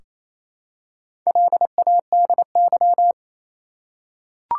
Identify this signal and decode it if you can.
Morse code, heard as a keyed tone, about 28 wpm, 700 Hz. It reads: LADY
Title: Four Letter Words - 28wpm